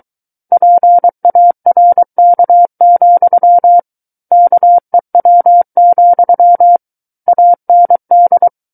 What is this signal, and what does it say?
Morse code, heard as a keyed tone, about 23 wpm, 700 Hz. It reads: PARK, KEW, AND
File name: Book One / 14-Chapter 14 - 23 WPM